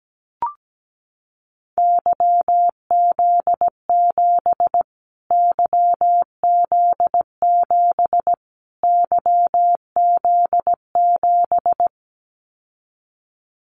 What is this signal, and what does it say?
Morse code, heard as a keyed tone, about 17 wpm, 700 Hz. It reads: YZ7 YZ7 YZ7
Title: Callsign Prefixes for Top 25 DXCC - Repeated 3x - 17wpm